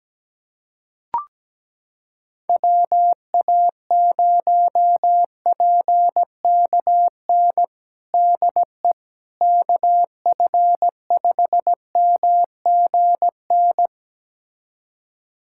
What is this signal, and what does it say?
Morse code, heard as a keyed tone, about 17 wpm, 700 Hz. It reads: WA0PKN DE KF5MGN
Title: USA Callsign DE Callsign - 6 Characters - 17wpm